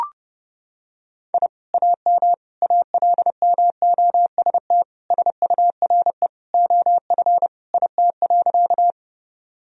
Morse code, heard as a keyed tone, about 30 wpm, 700 Hz. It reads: I AM ALMOST SURE OF IT.
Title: Sentences from Top 1000 Words - Review Part 4 - 30wpm